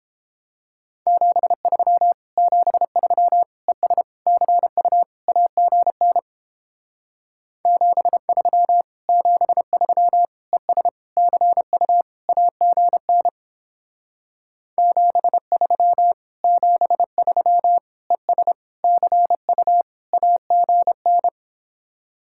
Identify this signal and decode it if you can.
Morse code, heard as a keyed tone, about 29 wpm, 700 Hz. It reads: 73 73 ES CU AGN 73 73 ES CU AGN 73 73 ES CU AGN